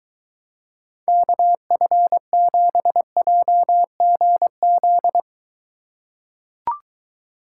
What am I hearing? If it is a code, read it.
Morse code, heard as a keyed tone, about 23 wpm, 700 Hz. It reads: KF7JGZ